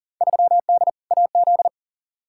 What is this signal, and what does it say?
Morse code, heard as a keyed tone, about 40 wpm, 700 Hz. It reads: 3D AZ
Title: Field Day Exchanges - 40wpm